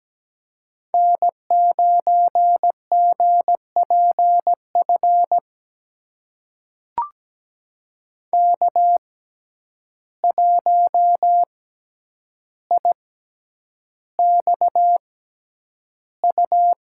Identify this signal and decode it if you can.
Morse code, heard as a keyed tone, about 17 wpm, spaced as for 9 wpm, 700 Hz. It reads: N9GPF K1IXU
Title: USA Call Signs - 5 Characters - 6x Character Spacing - 17wpm